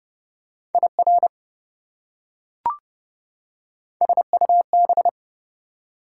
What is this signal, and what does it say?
Morse code, heard as a keyed tone, about 32 wpm, 700 Hz. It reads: IR SUB
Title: Prefixes and Suffixes 30wpm